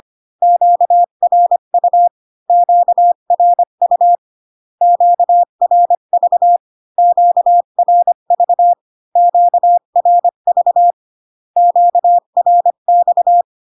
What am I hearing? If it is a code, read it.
Morse code, heard as a keyed tone, about 25 wpm, 700 Hz. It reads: QRU QRU QRV QRV QRV QRX